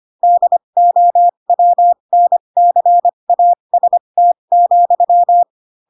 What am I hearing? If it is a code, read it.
Morse code, heard as a keyed tone, about 25 wpm, spaced as for 22 wpm, 700 Hz. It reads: DOWNCAST,
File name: Pinocchio / Pinocchio ch3 18 25 ik1hge